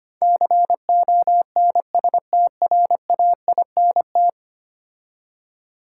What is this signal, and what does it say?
Morse code, heard as a keyed tone, about 25 wpm, 700 Hz. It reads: CONSTRAINT
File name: 10 Letter Words - First 3 Letters Given - 25wpm